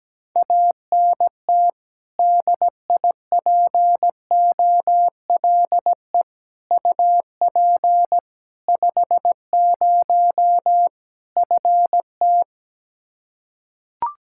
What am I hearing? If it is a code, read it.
Morse code, heard as a keyed tone, about 17 wpm, 700 Hz. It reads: ANT DIPOLE UP 50 FT
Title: Practice QSO - Exchange 2 - Info - 17wpm